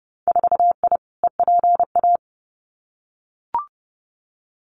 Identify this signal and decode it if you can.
Morse code, heard as a keyed tone, about 30 wpm, 700 Hz. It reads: 4I EPA
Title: Winter Field Day Exchanges Encore 30wpm